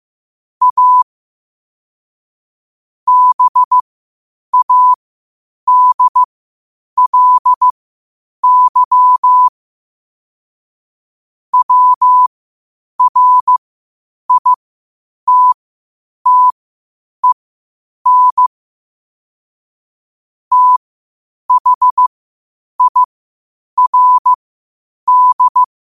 Morse code, heard as a keyed tone, about 15 wpm, spaced as for 8 wpm, 1000 Hz. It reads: A BADLY WRITTEN THIRD
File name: qotdfiles / 8wpm